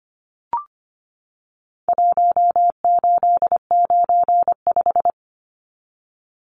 Morse code, heard as a keyed tone, about 25 wpm, 700 Hz. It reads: 1895
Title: Numbers - Contest Style - 4-Digit 25wpm